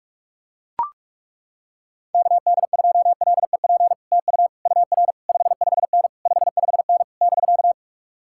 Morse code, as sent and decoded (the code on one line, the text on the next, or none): -.- -.. ..--- .-.. .--. / - ..- / ..- .-. / ..... ..... -. / ..... ..... -. / -...-.-
KD2LEP TU UR 55N 55N <BK>